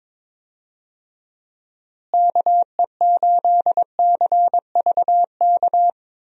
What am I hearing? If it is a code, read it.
Morse code, heard as a keyed tone, about 22 wpm, 700 Hz. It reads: KE8CVK
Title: USA Call Signs - 6 Characters - Repeated 2x - 22wpm